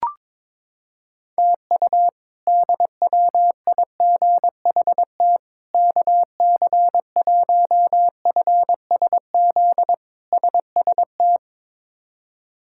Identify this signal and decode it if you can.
Morse code, heard as a keyed tone, about 22 wpm, 700 Hz. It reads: TU DWIGHT KC1FSZ SST
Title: K1USN SST QSO Practice - 22wpm